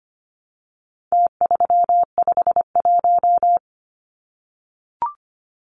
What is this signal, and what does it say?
Morse code, heard as a keyed tone, about 25 wpm, 700 Hz. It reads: T351
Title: Numbers - Contest Style - 4-Digit 25wpm